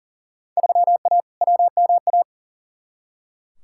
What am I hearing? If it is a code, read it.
Morse code, heard as a keyed tone, about 40 wpm, 700 Hz. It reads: 3A WMA
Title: Field Day Exchanges Encore - 40wpm